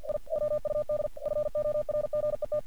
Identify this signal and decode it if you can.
Morse code, heard as a keyed tone, about 47 wpm, 600 Hz. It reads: I WUD FORGET
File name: archived / CRSnet